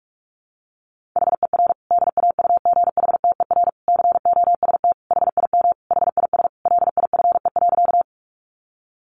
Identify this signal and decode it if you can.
Morse code, heard as a keyed tone, about 44 wpm, 700 Hz. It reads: HER DAUGHTER COST HIM HIS LIFE.
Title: Sentences from Top 1500 Words 45wpm